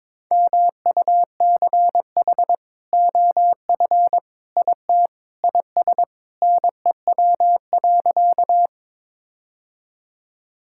Morse code, heard as a keyed tone, about 22 wpm, 700 Hz. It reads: MUCH OF IT IS NEW.
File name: practice / 22wpm / Sentences with 2-Word Phrases - 22wpm